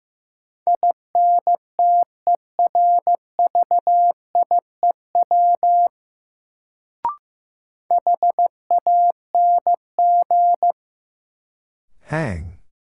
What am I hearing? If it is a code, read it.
Morse code, heard as a keyed tone, about 15 wpm, 700 Hz. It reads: INTERVIEW HANG
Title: Sentences from Top 1600 Words - New Word Drill - 15wpm